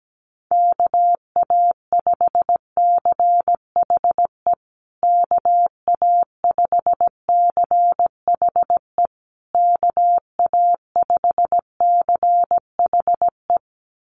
Morse code, heard as a keyed tone, about 17 wpm, 700 Hz. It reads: KA5CHE KA5CHE KA5CHE
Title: USA Call Signs Repeated 3x 17wpm